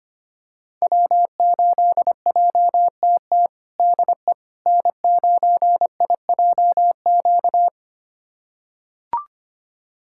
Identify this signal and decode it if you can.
Morse code, heard as a keyed tone, about 25 wpm, 700 Hz. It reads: W8JTT DE N9IJQ
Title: USA Callsign DE Callsign - 5 Characters - 25wpm